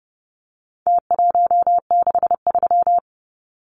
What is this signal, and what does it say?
Morse code, heard as a keyed tone, about 30 wpm, 700 Hz. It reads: T163
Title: Numbers - Contest Style - 4-Digit 30wpm